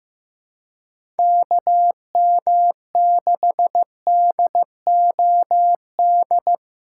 Morse code, heard as a keyed tone, about 15 wpm, 700 Hz. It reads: KM6DOD
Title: USA Call Signs - 2x Character Spacing - 15wpm